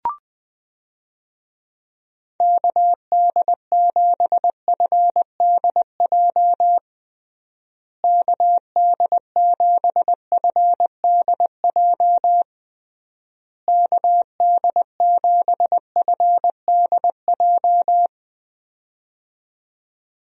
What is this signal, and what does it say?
Morse code, heard as a keyed tone, about 20 wpm, 700 Hz. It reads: KD7FDJ KD7FDJ KD7FDJ
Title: USA Call Signs - 6 Characters - Repeated 3x - 20wpm